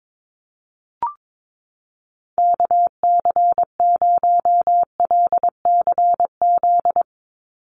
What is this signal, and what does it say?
Morse code, heard as a keyed tone, about 22 wpm, 700 Hz. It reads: KC0LCZ
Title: USA Call Signs 22wpm